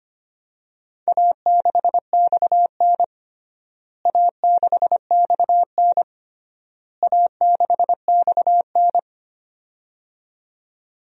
Morse code, heard as a keyed tone, about 25 wpm, 700 Hz. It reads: A6XN A6XN A6XN
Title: DXCC-100 Call Signs - 4 Characters - Repeated 3x - 25wpm